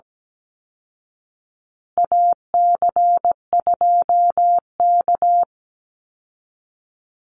Morse code, heard as a keyed tone, about 17 wpm, 700 Hz. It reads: AC2K
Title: Call Signs Repeated 3x - 17wpm